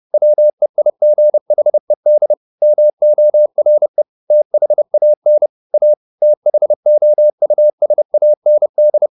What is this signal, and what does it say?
Morse code, heard as a keyed tone, about 30 wpm, 600 Hz. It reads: WEIGHED MORE THAN A THOUSAND
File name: the old man and the sea / the old man and the sea 1 30wpm